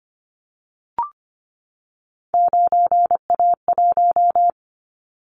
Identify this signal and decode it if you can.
Morse code, heard as a keyed tone, about 25 wpm, 700 Hz. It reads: 9A1
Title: Callsign Prefixes for Top 50 DXCC - 25wpm